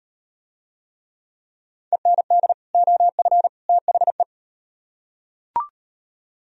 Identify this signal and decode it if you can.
Morse code, heard as a keyed tone, about 38 wpm, 700 Hz. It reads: END OF THE